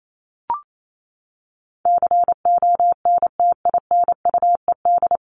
Morse code, heard as a keyed tone, about 28 wpm, 700 Hz. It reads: CONTINUED
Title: ED Suffix - Top 100 Words with Base Words 28wpm